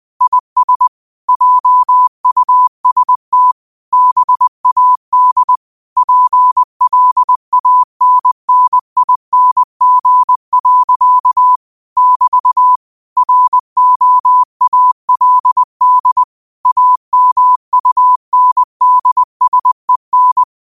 Morse code, heard as a keyed tone, about 20 wpm, 1000 Hz. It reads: IS JUST BAD PLANNING. <BT> ROALD AMUNDSEN
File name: qotdfiles / 20wpm